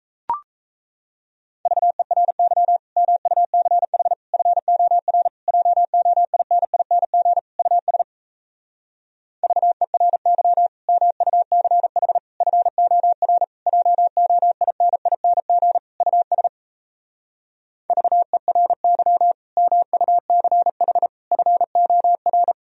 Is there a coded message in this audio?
Yes: Morse code from 1.7 to 22.5 s, heard as a keyed tone, about 37 wpm, 700 Hz. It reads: VERY MUCH FOR JOINING US VERY MUCH FOR JOINING US VERY MUCH FOR